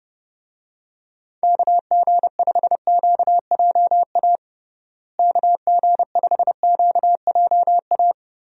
Morse code, heard as a keyed tone, about 30 wpm, 700 Hz. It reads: KG5QJA KG5QJA
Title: USA Call Signs - 6 Characters - Repeated 3x - 30wpm